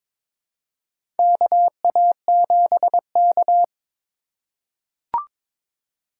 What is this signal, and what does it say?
Morse code, heard as a keyed tone, about 21 wpm, 700 Hz. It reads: KA7K
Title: USA Call Signs - 4 Characters - 2x Character Spacing - 22wpm